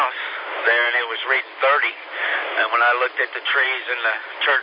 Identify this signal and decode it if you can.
Emergency Alert System (SAME) tones, heard as harsh buzzing bursts, with speech. It is heard from 0.6 to 4.2 s.